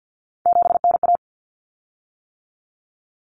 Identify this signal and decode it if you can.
Morse code, heard as a keyed tone, about 50 wpm, 700 Hz. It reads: 7NA
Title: Common Cut Numbers - 3-Digit 50wpm